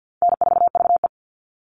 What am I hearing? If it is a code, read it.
Morse code, heard as a keyed tone, about 50 wpm, 700 Hz. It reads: N4VE
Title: All Cut Numbers - 4-Digit 50wpm